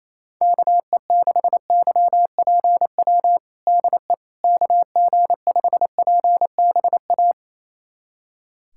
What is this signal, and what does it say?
Morse code, heard as a keyed tone, about 28 wpm, 700 Hz. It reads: KE6YPW DE KG5PBA
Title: USA Callsign DE Callsign - 6 Characters - 28wpm